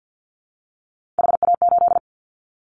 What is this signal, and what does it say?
Morse code, heard as a keyed tone, about 50 wpm, 700 Hz. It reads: HA8
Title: Callsign Prefixes for Top 50 DXCC - 50wpm